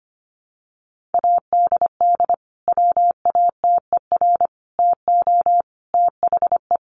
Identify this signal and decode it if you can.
Morse code, heard as a keyed tone, about 25 wpm, 700 Hz. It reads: ADD WATER TO THE
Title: Sentences from Top 1500 Words 25wpm